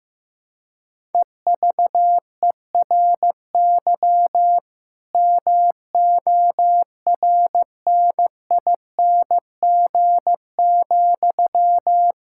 Morse code, heard as a keyed tone, about 15 wpm, 700 Hz. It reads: EVERY MORNING,
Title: Sentences from Top 1000 Words - Review Part 6 - 15wpm